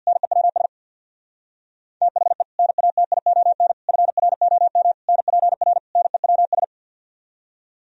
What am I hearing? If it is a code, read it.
Morse code, heard as a keyed tone, about 62 wpm, 700 Hz. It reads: NEWS THE NATION FROM NPR NEWS
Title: Speed-Racing - Sets of 5 Words 50wpm